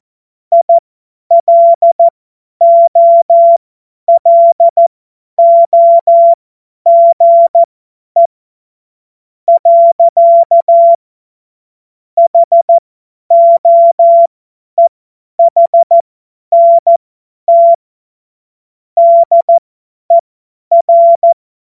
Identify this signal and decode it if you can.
Morse code, heard as a keyed tone, about 14 wpm, spaced as for 10 wpm, 680 Hz. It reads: ILOLOGE. HOEHNT DER
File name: Morsekurs / Witze / Witz 21. Juni 2022 50 F70